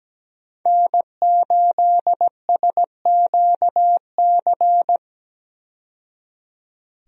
Morse code, heard as a keyed tone, about 17 wpm, 700 Hz. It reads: N8SQC
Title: USA Call Signs - 5 Characters - 17wpm